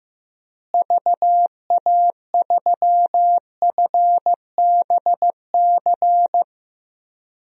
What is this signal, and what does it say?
Morse code, heard as a keyed tone, about 15 wpm, 700 Hz. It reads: VA3FBC
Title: DXCC-100 Call Signs - 6 Characters - Repeated 2x - 15wpm